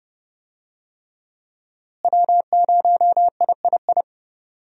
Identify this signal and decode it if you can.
Morse code, heard as a keyed tone, about 32 wpm, 700 Hz. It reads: W0III
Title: USA Call Signs - 5 Characters - Repeated 3x - 30wpm